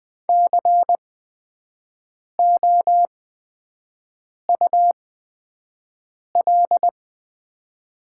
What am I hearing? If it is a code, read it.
Morse code, heard as a keyed tone, about 20 wpm, spaced as for 5 wpm, 700 Hz. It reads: COUL
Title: 8x Character Spacing - Top 500 Words - 20wpm